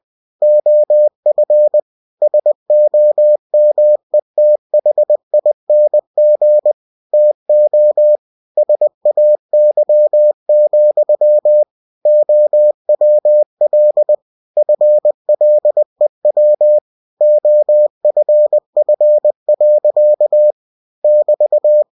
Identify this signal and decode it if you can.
Morse code, heard as a keyed tone, about 20 wpm, 600 Hz. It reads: OF SOMETHING TO SAY, OWL FLEW OFF. <BT>
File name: cw-books / winnie-the-pooh / winnie-20wpm-page-0099